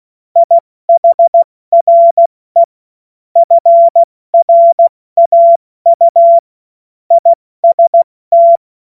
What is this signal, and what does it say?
Morse code, heard as a keyed tone, about 16 wpm, 680 Hz. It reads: IHRE FRAU IST
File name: Morsekurs / Witze / Witz 09. Dezember 2025 70 F80